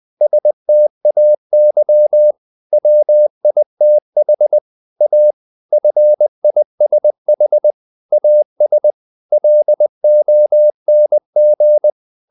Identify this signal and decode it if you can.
Morse code, heard as a keyed tone, about 20 wpm, 600 Hz. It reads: STAY WITH A FISH AS LONG
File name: the old man and the sea / the old man and the sea part 1 20wpm